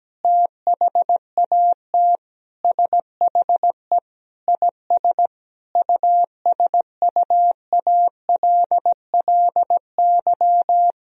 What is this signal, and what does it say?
Morse code, heard as a keyed tone, about 17 wpm, 700 Hz. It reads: THAT SHE IS USUALLY
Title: Sentences from Top 1000 Words - Review Part 3 - 17wpm